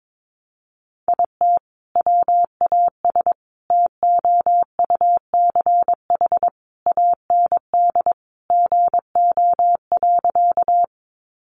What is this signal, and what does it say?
Morse code, heard as a keyed tone, about 22 wpm, 700 Hz. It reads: IT WAS TOUCH AND GO.
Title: Sentences from Top 1200 Words 22wpm